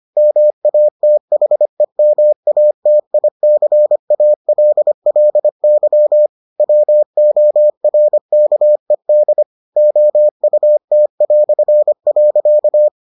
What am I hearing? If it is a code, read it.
Morse code, heard as a keyed tone, about 25 wpm, 600 Hz. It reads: MATHEMATICALLY WORKED OUT".